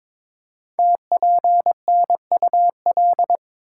Morse code, heard as a keyed tone, about 22 wpm, 700 Hz. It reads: TPNUL
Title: Code Groups - 5 Letters - 22wpm